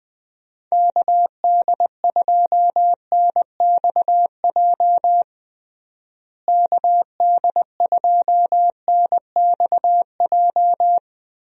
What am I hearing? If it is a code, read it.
Morse code, heard as a keyed tone, about 20 wpm, 700 Hz. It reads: KD2NXJ KD2NXJ
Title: USA Call Signs - 6 Characters - Repeated 2x - 20wpm